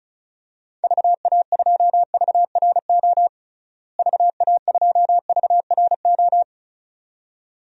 Morse code, heard as a keyed tone, about 36 wpm, 700 Hz. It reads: VA2VRO VA2VRO
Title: DXCC-100 Call Signs - 6 Characters - Repeated 2x - 35wpm